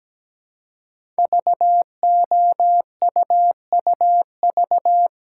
Morse code, heard as a keyed tone, about 17 wpm, 700 Hz. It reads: VOUUV
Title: Code Groups - 5 Letters - 17wpm